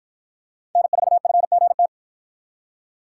Morse code, heard as a keyed tone, about 53 wpm, 700 Hz. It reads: N4FGT